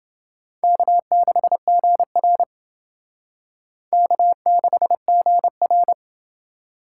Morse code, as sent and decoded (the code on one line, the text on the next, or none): -.- -.... --. .-. / -.- -.... --. .-.
K6GR K6GR